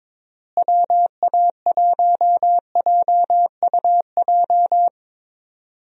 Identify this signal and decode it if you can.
Morse code, heard as a keyed tone, about 22 wpm, 700 Hz. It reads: WA1JUJ
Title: USA Call Signs - 6 Characters - Repeated 3x - 22wpm